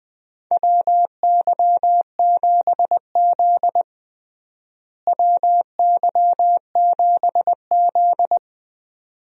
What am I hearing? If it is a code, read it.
Morse code, heard as a keyed tone, about 21 wpm, 700 Hz. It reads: WY7Z WY7Z
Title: USA Call Signs - 4 Characters - Repeated 2x - 20wpm